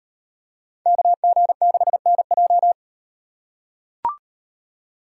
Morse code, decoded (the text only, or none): KG6NJ